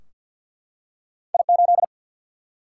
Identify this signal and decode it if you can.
Morse code, heard as a keyed tone, about 53 wpm, 700 Hz. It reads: I8
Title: Single-Letter Single-Number 50wpm